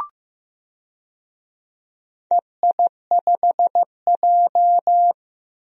Morse code, heard as a keyed tone, about 15 wpm, 700 Hz. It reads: EI5J
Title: DXCC-100 Call Signs - 4 Characters - Repeated 2x - 15wpm